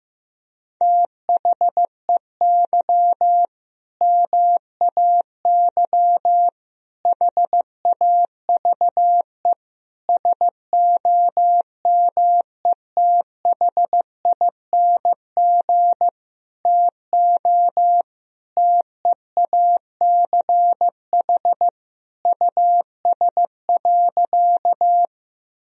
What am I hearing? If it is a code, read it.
Morse code, heard as a keyed tone, about 15 wpm, 700 Hz. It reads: THEY MAY HAVE SOMETHING TO TEACH US.
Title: Sentences from Top 1000 Words - Review Part 2 - 15wpm